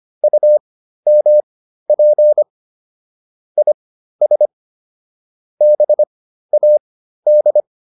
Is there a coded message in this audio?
Yes: Morse code from 0.2 to 7.6 s, heard as a keyed tone, about 25 wpm, spaced as for 13 wpm, 600 Hz. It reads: UMP IS BAD